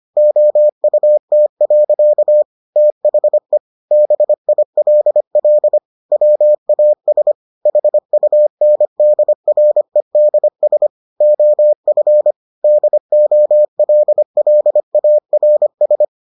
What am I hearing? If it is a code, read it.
Morse code, heard as a keyed tone, about 25 wpm, 600 Hz. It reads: OUT. THE BILL WAS HUNDREDS OF DOLLARS